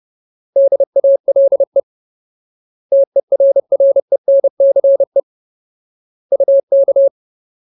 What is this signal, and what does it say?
Morse code, heard as a keyed tone, about 30 wpm, 550 Hz. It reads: DALE TERRENCE UK
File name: Ultimate Word List / 30WPM Ultimate